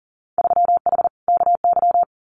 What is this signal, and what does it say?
Morse code, heard as a keyed tone, about 40 wpm, 700 Hz. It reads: 3H KY
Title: Winter Field Day Exchanges Encore 40wpm